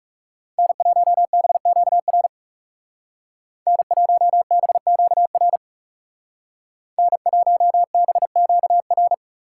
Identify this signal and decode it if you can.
Morse code, heard as a keyed tone, about 36 wpm, 700 Hz. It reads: N1BQR N1BQR N1BQR
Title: Speed Racing - USA Call Signs 30wpm